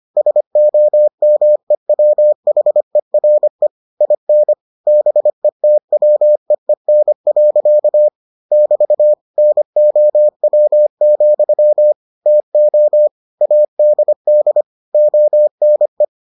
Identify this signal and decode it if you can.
Morse code, heard as a keyed tone, about 25 wpm, 600 Hz. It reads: SOMEWHERE IN BETWEEN. <BT> NOW, TO ADD ONE